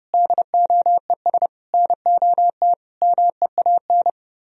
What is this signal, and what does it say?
Morse code, heard as a keyed tone, about 30 wpm, 700 Hz. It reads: DOES NOT MEAN